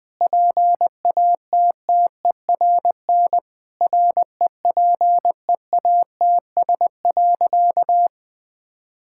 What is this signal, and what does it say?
Morse code, heard as a keyed tone, about 20 wpm, 700 Hz. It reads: PATTERN REPEATS.
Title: Sentences with 6-Word Phrases - 20wpm